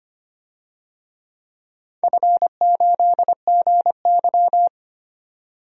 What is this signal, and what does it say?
Morse code, heard as a keyed tone, about 25 wpm, 700 Hz. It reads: F8GY
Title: DXCC-100 Call Signs - 4 Characters - 25wpm